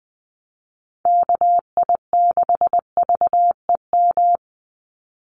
Morse code, heard as a keyed tone, about 20 wpm, 700 Hz. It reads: KI6VEM
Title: USA Call Signs 20wpm